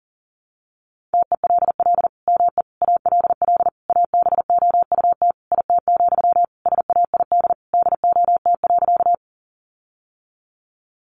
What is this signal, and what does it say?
Morse code, heard as a keyed tone, about 40 wpm, 700 Hz. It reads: TELL ME ALL ABOUT IT, SAID DOT.
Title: Sentences from Top 1500 Words 40wpm